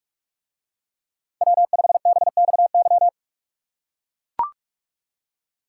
Morse code, heard as a keyed tone, about 44 wpm, 700 Hz. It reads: W5BXY